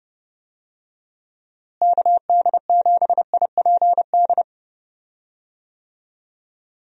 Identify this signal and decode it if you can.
Morse code, heard as a keyed tone, about 30 wpm, 700 Hz. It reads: KD7IPD
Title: USA Call Signs - 6 Characters - Repeated 2x - 30wpm